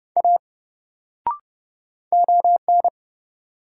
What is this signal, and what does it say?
Morse code, heard as a keyed tone, about 29 wpm, 700 Hz. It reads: A ON
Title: Sentences from Top 100 Words - New Word Drill - 30wpm